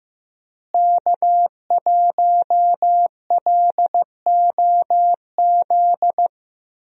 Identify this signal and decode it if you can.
Morse code, heard as a keyed tone, about 15 wpm, 700 Hz. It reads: K1LOZ
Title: USA Call Signs - 5 Characters - Repeated 3x - 15wpm